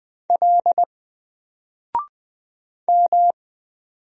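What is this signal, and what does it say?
Morse code, heard as a keyed tone, about 20 wpm, 700 Hz. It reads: L M
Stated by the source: Single letter and number Morse drill at 20 wpm, from N4LCD.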